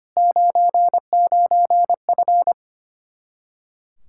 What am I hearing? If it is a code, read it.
Morse code, heard as a keyed tone, about 24 wpm, 700 Hz. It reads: 99F
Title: Speed-Racing Common QSO Phrases - 25wpm